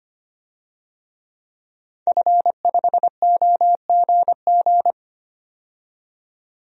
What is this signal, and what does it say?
Morse code, heard as a keyed tone, about 25 wpm, 700 Hz. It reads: F5OGG
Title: DXCC-100 Call Signs - 5 Characters - 25wpm